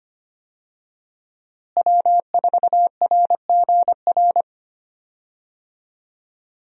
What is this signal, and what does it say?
Morse code, heard as a keyed tone, about 25 wpm, 700 Hz. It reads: W4RGR
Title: USA Call Signs - 5 Characters - Repeated 3x - 25wpm